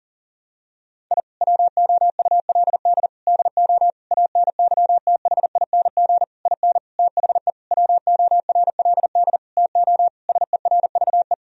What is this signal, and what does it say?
Morse code, heard as a keyed tone, about 40 wpm, 700 Hz. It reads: I WOULD DO ANYTHING IN THE WORLD TO SERVE
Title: Sentences from Top 1000 Words - Review Part 2 - 40wpm